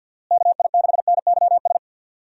Morse code, heard as a keyed tone, about 46 wpm, 700 Hz. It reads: KI6NYS
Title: USA Call Signs - 6 Characters - Repeated 3x - 50wpm